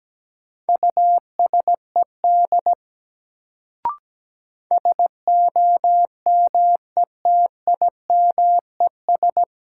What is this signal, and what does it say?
Morse code, heard as a keyed tone, about 17 wpm, 700 Hz. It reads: USED SOMETIMES
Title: Sentences from Top 600 Words - New Word Drill - 17wpm